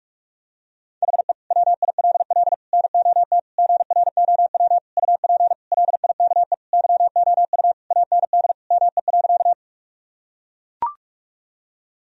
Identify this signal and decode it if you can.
Morse code, heard as a keyed tone, about 41 wpm, 700 Hz. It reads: HE WILL NOT GROW UP LIKE YOU AND ME.
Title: Sentences from Top 1000 Words - Review Part 1 - 45wpm